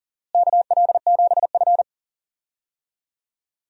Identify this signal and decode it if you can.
Morse code, heard as a keyed tone, about 40 wpm, 700 Hz. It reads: KL7F
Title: DXCC-100 Call Signs - 4 Characters - Repeated 3x - 40wpm